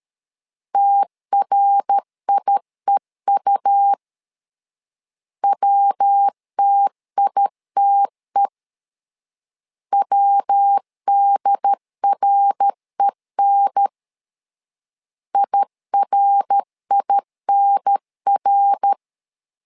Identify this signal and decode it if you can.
Morse code, heard as a keyed tone, about 13 wpm, 800 Hz. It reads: TRIEU WTITE WDREN IRINR
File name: LEZIONI / 3 MWRO / 3 MWR 12wpm 1711